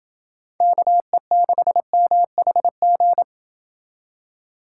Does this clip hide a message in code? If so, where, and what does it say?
Yes: Morse code from 0.6 to 3.2 s, heard as a keyed tone, about 29 wpm, 700 Hz. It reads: KE6MHG